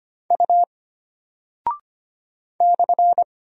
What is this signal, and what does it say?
Morse code, heard as a keyed tone, about 25 wpm, 700 Hz. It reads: U /
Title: Single Character (Letter:Number:Punctuation:Prosign) Alt 25wpm